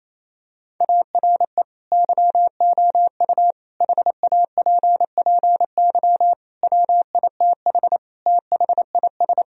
Morse code, heard as a keyed tone, about 28 wpm, 700 Hz. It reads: ARE YOU HAPPY WITH THIS
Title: Sentences from Top 1000 Words - Review Part 2 - 28wpm